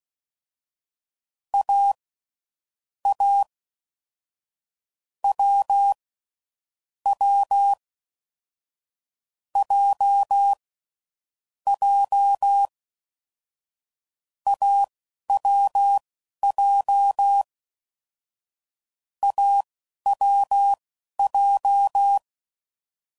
Morse code, heard as a keyed tone, about 16 wpm, spaced as for 11 wpm, 800 Hz. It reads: A A W W J J AWJ AWJ
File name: Mixed / Short Runs / awjndb6gz7